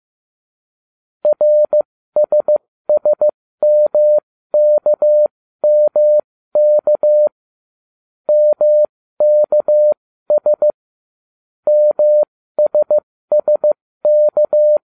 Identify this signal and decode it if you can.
Morse code, heard as a keyed tone, about 15 wpm, spaced as for 13 wpm, 600 Hz. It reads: RSSMKMK MKS MSSK